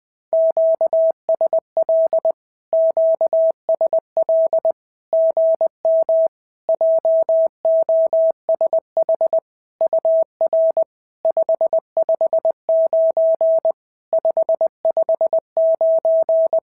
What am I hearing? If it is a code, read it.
Morse code, heard as a keyed tone, about 20 wpm, 650 Hz. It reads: QSL QSL GM JOSH UR 559 559
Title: SOTA QSO Practice 20wpm